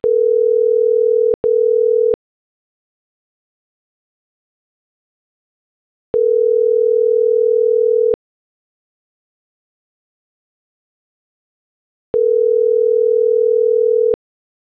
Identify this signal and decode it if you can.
Ringback tone (North America), 440 + 480 Hz.